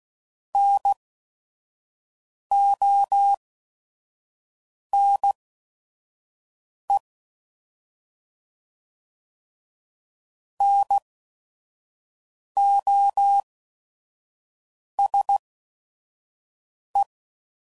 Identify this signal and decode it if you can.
Morse code, heard as a keyed tone, about 16 wpm, spaced as for 5 wpm, 800 Hz. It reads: NONE NOSE
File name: Prefixes / By First Letter / Ns